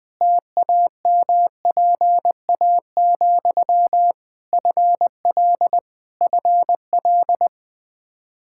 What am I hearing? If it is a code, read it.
Morse code, heard as a keyed tone, about 21 wpm, 700 Hz. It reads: TAMPA, FL FL
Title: Practice QSO - Exchange 1 - Beginning - 20wpm